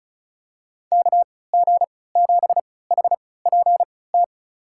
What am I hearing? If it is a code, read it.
Morse code, heard as a keyed tone, about 36 wpm, spaced as for 20 wpm, 700 Hz. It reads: KG7HPT
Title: USA Call Signs - 6 Characters - 3x Character Spacing - 35wpm